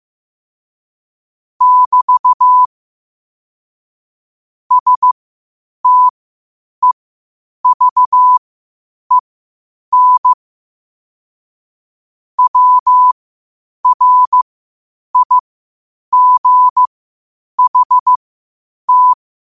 Morse code, heard as a keyed tone, about 15 wpm, spaced as for 8 wpm, 1000 Hz. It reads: <BT> STEVEN WRIGHT
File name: qotdfiles / 8wpm